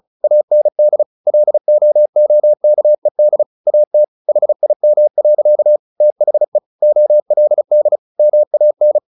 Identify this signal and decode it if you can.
Morse code, heard as a keyed tone, about 34 wpm, 600 Hz. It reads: AND LOOKED AT HIM. THE OLD MAN
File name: the old man and the sea / the old man and the sea 1 35wpm